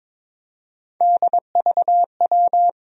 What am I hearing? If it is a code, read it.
Morse code, heard as a keyed tone, about 21 wpm, 700 Hz. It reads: DVW
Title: USA Call Sign Postfixes - 3 Letters 22wpm